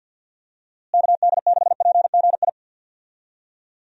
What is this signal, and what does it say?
Morse code, heard as a keyed tone, about 50 wpm, 700 Hz. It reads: KD6PGI
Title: USA Call Signs - 6 Characters - Repeated 3x - 50wpm